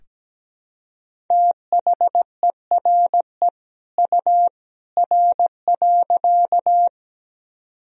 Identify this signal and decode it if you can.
Morse code, heard as a keyed tone, about 17 wpm, 700 Hz. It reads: THERE U R.
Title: Sentences from Top 100 Words and Abbreviations 17wpm